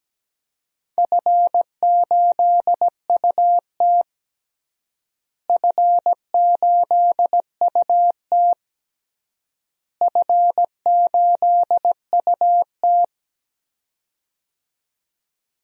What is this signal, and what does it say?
Morse code, heard as a keyed tone, about 17 wpm, 700 Hz. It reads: F8UT F8UT F8UT
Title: DXCC-100 Call Signs - 4 Characters - Repeated 3x - 17wpm